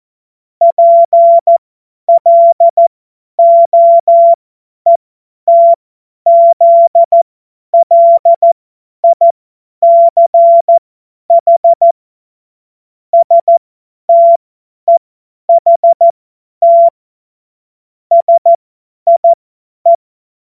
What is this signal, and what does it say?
Morse code, heard as a keyed tone, about 14 wpm, spaced as for 10 wpm, 680 Hz. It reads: PLOETZLICH STEHT SIE
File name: Morsekurs / Witze / Witz 19. Oktober 2021, 50 F70